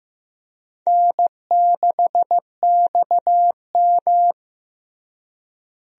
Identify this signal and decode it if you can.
Morse code, heard as a keyed tone, about 15 wpm, 700 Hz. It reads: N6XM
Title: USA Call Signs - 4 Characters - Repeated 2x - 15wpm